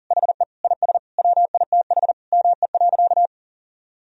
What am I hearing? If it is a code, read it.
Morse code, heard as a keyed tone, about 40 wpm, 700 Hz. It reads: HE IS WITH ME.
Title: Sentences from Top 1000 Words - Review Part 3 - 40wpm